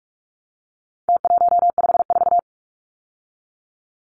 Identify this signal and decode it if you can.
Morse code, heard as a keyed tone, about 44 wpm, 700 Hz. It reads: T154
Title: Numbers - Contest Style - 4-Digit 45wpm